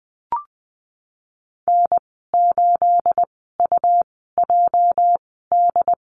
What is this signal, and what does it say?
Morse code, heard as a keyed tone, about 20 wpm, spaced as for 14 wpm, 700 Hz. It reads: N8UJD
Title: 238 Lesson 38g - Letter q - Call Signs 20wpm